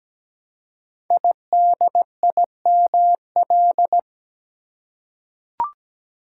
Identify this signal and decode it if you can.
Morse code, heard as a keyed tone, about 17 wpm, 690 Hz. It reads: IDIML
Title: Code Groups - 5 Letter:Numbers - 17wpm